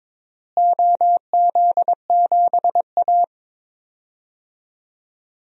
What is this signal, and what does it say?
Morse code, heard as a keyed tone, about 22 wpm, 700 Hz. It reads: OZ7A
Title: DXCC-100 Call Signs - 4 Characters - Repeated 3x - 22wpm